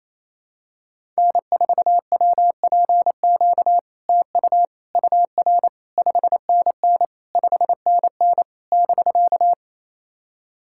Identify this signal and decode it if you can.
Morse code, heard as a keyed tone, about 28 wpm, 700 Hz. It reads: N4WPQ TU UR 5NN 5NN <BK>
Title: SOTA QSO Practice 28wpm